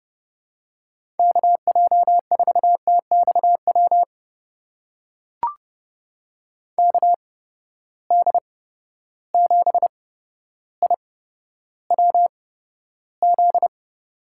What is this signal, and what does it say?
Morse code, heard as a keyed tone, about 30 wpm, spaced as for 13 wpm, 700 Hz. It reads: KJ4TXW KD7IWZ
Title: USA Call Signs - 6 Characters - 8x Character Spacing - 30wpm